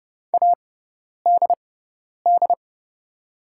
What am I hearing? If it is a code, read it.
Morse code, heard as a keyed tone, about 29 wpm, spaced as for 10 wpm, 700 Hz. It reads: ADD
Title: 6x Character Spacing - Top 500 Words - 30wpm